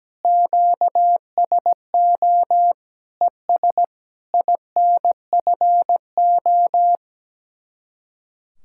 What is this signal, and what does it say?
Morse code, heard as a keyed tone, about 17 wpm, 700 Hz. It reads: QSO ES INFO
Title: Speed-Racing Common QSO Phrases - 17wpm